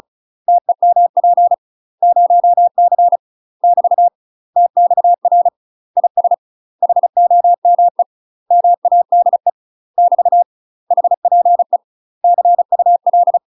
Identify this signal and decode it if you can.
Morse code, heard as a keyed tone, about 35 wpm, 700 Hz. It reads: TEMP 0C <BT> TXR IS HOME MADE <BT> HPE CUL
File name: UK MASTER QSOs TRAINING / 35WPM / UK MASTER QSOs 0041